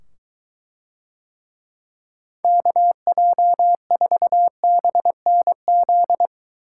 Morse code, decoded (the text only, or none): KJ4BNZ